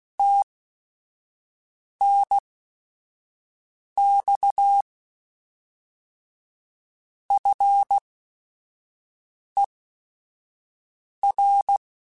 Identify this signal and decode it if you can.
Morse code, heard as a keyed tone, about 16 wpm, spaced as for 5 wpm, 800 Hz. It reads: TNX FER